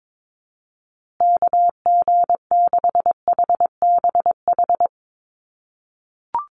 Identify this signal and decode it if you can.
Morse code, heard as a keyed tone, about 22 wpm, 700 Hz. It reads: KG6HBH
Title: USA Call Signs Repeated 2x 22wpm